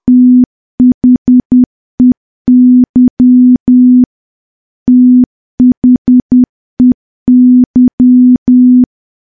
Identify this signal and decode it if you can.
Morse code, heard as a keyed tone, about 10 wpm, 250 Hz. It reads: THEY THEY
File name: morse / morse-training-10WPM